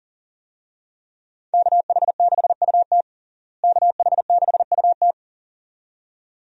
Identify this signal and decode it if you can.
Morse code, heard as a keyed tone, about 40 wpm, 700 Hz. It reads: KH6UT KH6UT
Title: USA Call Signs - 5 Characters - Repeated 2x - 40wpm